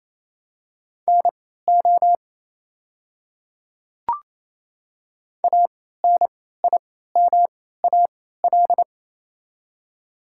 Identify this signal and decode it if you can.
Morse code, heard as a keyed tone, about 28 wpm, spaced as for 16 wpm, 700 Hz. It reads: NO ANIMAL
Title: 3x Character Spacing - Top 200 Words - 28wpm